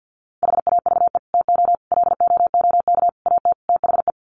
Morse code, heard as a keyed tone, about 50 wpm, 700 Hz. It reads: HAVE TO LOOK AT THE
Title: Sets of 5 Words Encore 50wpm